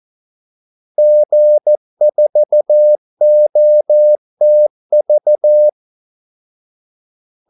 Morse code, heard as a keyed tone, about 14 wpm, 600 Hz. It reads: G4OTV
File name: Callsigns-14wpm